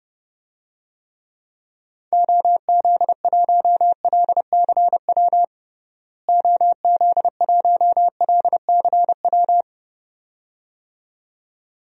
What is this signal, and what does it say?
Morse code, heard as a keyed tone, about 31 wpm, 700 Hz. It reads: OZ1LCW OZ1LCW
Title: DXCC-100 Call Signs - 6 Characters - Repeated 2x - 30wpm